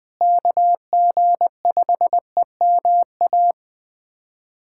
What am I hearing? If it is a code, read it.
Morse code, heard as a keyed tone, about 21 wpm, 700 Hz. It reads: KG5EMA
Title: USA Callsign DE Callsign - 6 Characters - 20wpm